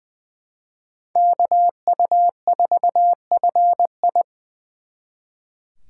Morse code, heard as a keyed tone, about 24 wpm, 700 Hz. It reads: KU4FI
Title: Speed Racing - USA 2x2 Call Signs 20wpm